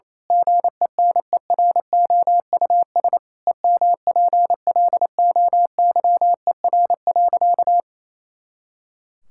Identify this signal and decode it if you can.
Morse code, heard as a keyed tone, about 28 wpm, 700 Hz. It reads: GENEROUS EMPLOYER.
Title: Pangram Challenge v2 - 28wpm